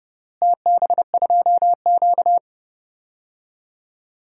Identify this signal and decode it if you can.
Morse code, heard as a keyed tone, about 31 wpm, 700 Hz. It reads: TB2Q
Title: DXCC-100 Call Signs - 4 Characters - Repeated 3x - 30wpm